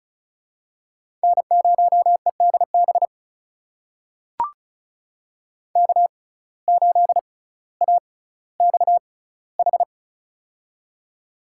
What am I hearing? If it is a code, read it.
Morse code, heard as a keyed tone, about 36 wpm, spaced as for 18 wpm, 700 Hz. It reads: N0EDB K8AXH
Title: USA Call Signs - 5 Characters - 6x Character Spacing - 35wpm